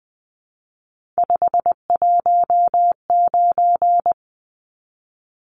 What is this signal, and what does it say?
Morse code, heard as a keyed tone, about 20 wpm, 700 Hz. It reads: 519
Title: Numbers - 3-Digit 20wpm